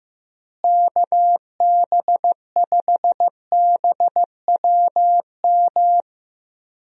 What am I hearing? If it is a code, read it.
Morse code, heard as a keyed tone, about 15 wpm, 700 Hz. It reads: KB5BWM
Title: USA Call Signs - 6 Characters - Repeated 2x - 15wpm